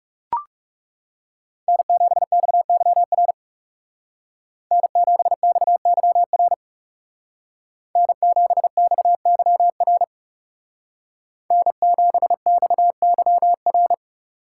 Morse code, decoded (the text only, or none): N7XYR N7XYR N7XYR N7XYR